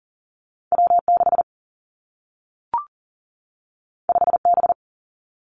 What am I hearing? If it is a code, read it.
Morse code, heard as a keyed tone, about 39 wpm, 700 Hz. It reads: W6 5B
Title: Two Challenging Letter-Number Combinations 40wpm